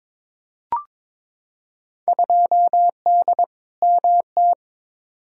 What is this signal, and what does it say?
Morse code, heard as a keyed tone, about 22 wpm, 700 Hz. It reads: 2D MT
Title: Field Day Exchanges Encore - 22wpm